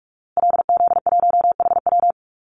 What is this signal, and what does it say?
Morse code, heard as a keyed tone, about 44 wpm, 700 Hz. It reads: LZ1HW
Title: Call Signs Repeated 2x - 45wpm